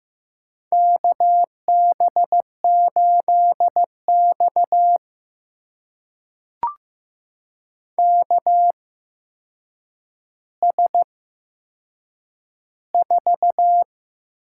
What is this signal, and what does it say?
Morse code, heard as a keyed tone, about 15 wpm, spaced as for 7 wpm, 700 Hz. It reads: KB8X KS4
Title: USA Call Signs - 8x Character Spacing - 15wpm